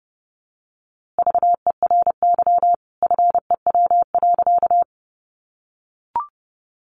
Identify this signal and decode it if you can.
Morse code, heard as a keyed tone, about 28 wpm, 700 Hz. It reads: VERY FEW.
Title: A Sound Of Thunder 30wpm